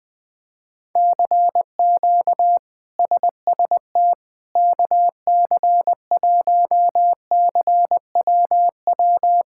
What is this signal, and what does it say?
Morse code, heard as a keyed tone, about 20 wpm, 700 Hz. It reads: CQ SST KC1CWW
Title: K1USN SST QSO Practice - 20wpm